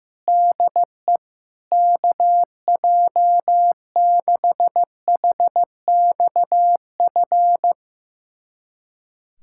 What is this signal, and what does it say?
Morse code, heard as a keyed tone, about 15 wpm, 700 Hz. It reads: DE KJ6HXF
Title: USA Callsign DE Callsign 15wpm